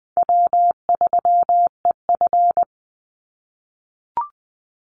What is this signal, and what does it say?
Morse code, heard as a keyed tone, about 20 wpm, 700 Hz. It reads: W3EF
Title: Easy Call Signs - 20wpm